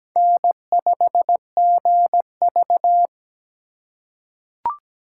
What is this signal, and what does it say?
Morse code, heard as a keyed tone, about 17 wpm, 700 Hz. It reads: N5GV
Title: USA Call Signs - 4 Characters - 4x Character Spacing - 17wpm